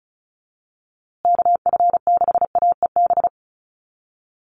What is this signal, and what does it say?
Morse code, heard as a keyed tone, about 34 wpm, 700 Hz. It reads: KF6AEB
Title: USA Call Signs Repeated 3x 35wpm